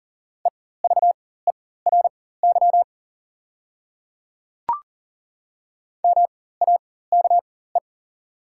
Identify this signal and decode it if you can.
Morse code, heard as a keyed tone, about 38 wpm, spaced as for 18 wpm, 690 Hz. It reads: EVERY MAKE
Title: 4x Character Spacing - Top 300 Words - 40wpm